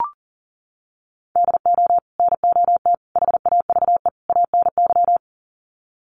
Morse code, heard as a keyed tone, about 40 wpm, 700 Hz. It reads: DO NOT HAVE ANY
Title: Sets of 4 Words 40wpm